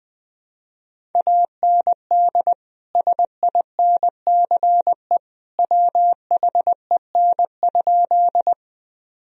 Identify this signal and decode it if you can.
Morse code, heard as a keyed tone, about 20 wpm, 700 Hz. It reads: AND SINCE WHEN?
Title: Sentences from Top 1000 Words - Review Part 1 - 20wpm